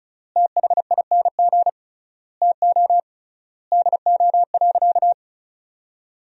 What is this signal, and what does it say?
Morse code, heard as a keyed tone, about 36 wpm, 700 Hz. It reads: THING TO DO.
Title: 3x Word-Spacing - Sentences from Top 100 Words - 35wpm